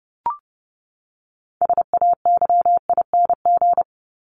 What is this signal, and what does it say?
Morse code, heard as a keyed tone, about 30 wpm, 700 Hz. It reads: SAYING
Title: ING Suffix - Top 100 Words with Base Words 30wpm